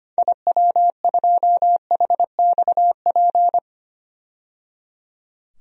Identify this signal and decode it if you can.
Morse code, heard as a keyed tone, about 25 wpm, 700 Hz. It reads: IW2HXP
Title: DXCC-100 Call Signs - 6 Characters - 25wpm